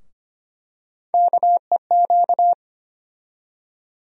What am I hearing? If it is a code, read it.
Morse code, heard as a keyed tone, about 25 wpm, 700 Hz. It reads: KEQ
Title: USA Call Sign Postfixes - 3 Letters 25wpm